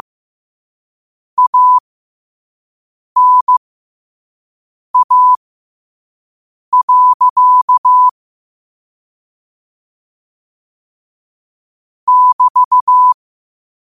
Morse code, heard as a keyed tone, about 15 wpm, spaced as for 5 wpm, 1000 Hz. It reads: ANA. <BT>
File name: qotdfiles / 5wpm